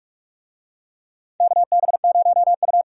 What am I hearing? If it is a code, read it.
Morse code, heard as a keyed tone, about 46 wpm, 700 Hz. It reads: KB0U
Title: USA Call Signs - 4 Characters - 45wpm